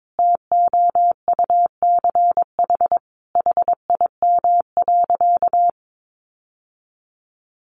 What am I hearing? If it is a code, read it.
Morse code, heard as a keyed tone, about 22 wpm, 700 Hz. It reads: TOUCH HIM.
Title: Sentences from Top 1200 Words 22wpm